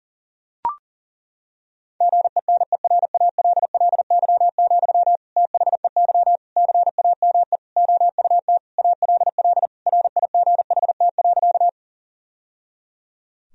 Morse code, heard as a keyed tone, about 37 wpm, 700 Hz. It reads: GENERALLY, THEY CAME OUT ALL RIGHT.
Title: Sentences from Top 1000 Words - Review Part 4 - 40wpm